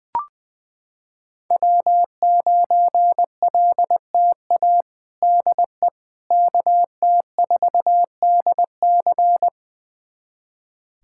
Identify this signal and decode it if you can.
Morse code, heard as a keyed tone, about 21 wpm, 700 Hz. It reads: W9LTA DE KT4DC
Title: USA Callsign DE Callsign - 5 Characters - 20wpm